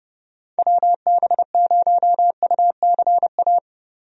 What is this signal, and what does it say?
Morse code, heard as a keyed tone, about 30 wpm, 700 Hz. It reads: WB0UCA
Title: USA Call Signs - 6 Characters - Repeated 2x - 30wpm